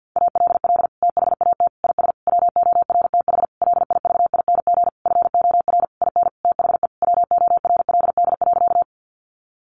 Morse code, heard as a keyed tone, about 50 wpm, 700 Hz. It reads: ALL THAT IS WORTH LIVING FOR IN THE WORLD.
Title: Sentences from Top 1100 Words 50wpm